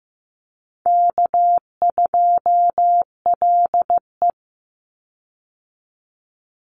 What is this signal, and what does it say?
Morse code, heard as a keyed tone, about 15 wpm, 700 Hz. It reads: K2LE
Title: Easy Call Signs Repeated 3x - 15wpm